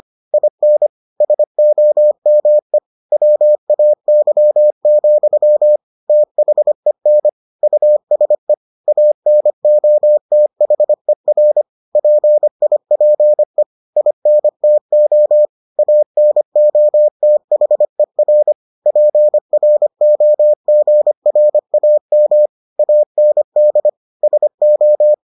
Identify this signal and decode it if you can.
Morse code, heard as a keyed tone, about 25 wpm, 600 Hz. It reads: IN SOME WAY, THEN USE ANOTHER PIPE INTO ANOTHER PROGRAM AND SO